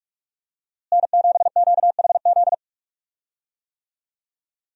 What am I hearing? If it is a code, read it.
Morse code, heard as a keyed tone, about 46 wpm, 700 Hz. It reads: N7QHZ
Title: USA Call Signs - 5 Characters - Repeated 2x - 45wpm